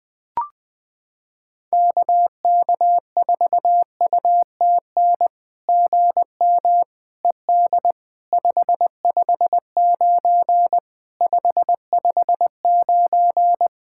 Morse code, heard as a keyed tone, about 20 wpm, 700 Hz. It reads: KK4UTN GM ED 559 559
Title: SOTA QSO Practice 20wpm